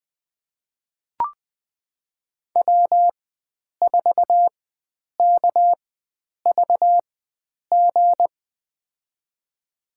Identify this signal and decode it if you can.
Morse code, heard as a keyed tone, about 20 wpm, spaced as for 9 wpm, 700 Hz. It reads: W4KVG
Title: USA Call Signs - 5 Characters - 4x Character Spacing - 20wpm